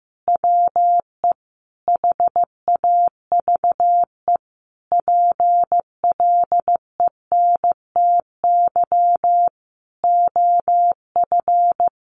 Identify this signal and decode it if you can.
Morse code, heard as a keyed tone, about 14 wpm, 700 Hz. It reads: WE HAVE PLENTY OF
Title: Sentences from Top 1300 Words 15wpm